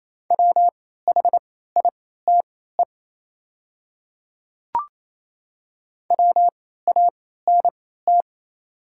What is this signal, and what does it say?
Morse code, heard as a keyed tone, about 28 wpm, spaced as for 16 wpm, 700 Hz. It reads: WHITE WANT
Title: 3x Character Spacing - Top 300 Words - 28wpm